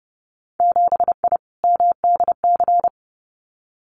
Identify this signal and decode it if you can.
Morse code, heard as a keyed tone, about 30 wpm, 700 Hz. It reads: 7I MDC
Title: Winter Field Day Exchanges Encore 30wpm